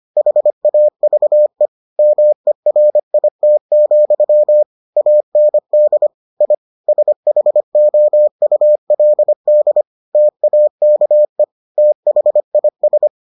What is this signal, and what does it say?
Morse code, heard as a keyed tone, about 25 wpm, 600 Hz. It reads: HAVE MERIT, AND I SHOULD TAKE THIS